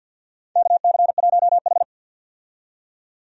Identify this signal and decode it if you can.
Morse code, heard as a keyed tone, about 49 wpm, 700 Hz. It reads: KC1H
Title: USA Call Signs - 4 Characters - 50wpm